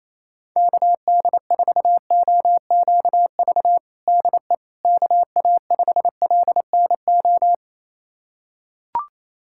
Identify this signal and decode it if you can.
Morse code, heard as a keyed tone, about 27 wpm, 700 Hz. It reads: KD4OQV DE KA5LNO
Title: USA Callsign DE Callsign - 6 Characters - 28wpm